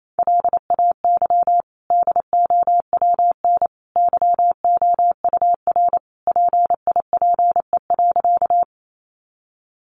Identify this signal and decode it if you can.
Morse code, heard as a keyed tone, about 28 wpm, 700 Hz. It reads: LAY DOWN YOUR PIPE.
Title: Sentences from Top 1500 Words 28wpm